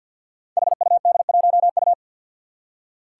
Morse code, heard as a keyed tone, about 48 wpm, 700 Hz. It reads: HUD1U
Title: Five Challenging Letter-Number Combinations 50wpm